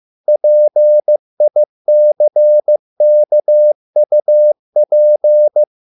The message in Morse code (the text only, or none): PICKUP